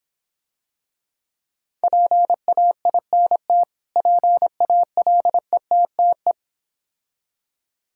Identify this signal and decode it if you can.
Morse code, heard as a keyed tone, about 26 wpm, 700 Hz. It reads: PAINT PALETTE